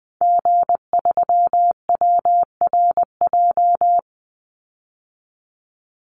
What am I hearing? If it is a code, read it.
Morse code, heard as a keyed tone, about 20 wpm, 700 Hz. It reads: G3WRJ
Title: Call Signs - 20wpm